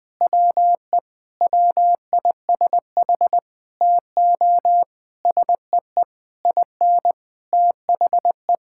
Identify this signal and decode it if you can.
Morse code, heard as a keyed tone, about 21 wpm, 700 Hz. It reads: WE WISH TO SEE IN THE
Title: Sentences with 8-Word Phrases - Spoken Beginning - 20wpm